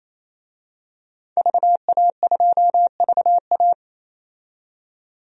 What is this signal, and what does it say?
Morse code, heard as a keyed tone, about 28 wpm, 700 Hz. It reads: VA2VA
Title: DXCC-100 Call Signs - 5 Characters - 28wpm